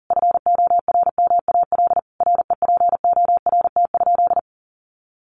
Morse code, heard as a keyed tone, about 37 wpm, 700 Hz. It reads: FORMAL REPORT?
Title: Sentences from Top 1300 Words 40wpm